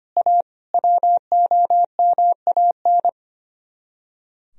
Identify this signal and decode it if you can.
Morse code, heard as a keyed tone, about 32 wpm, 700 Hz. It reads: A WOMAN